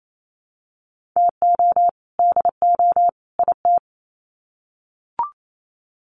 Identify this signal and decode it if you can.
Morse code, heard as a keyed tone, about 28 wpm, 700 Hz. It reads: TO DO IT
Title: Sets of 3 Words 28wpm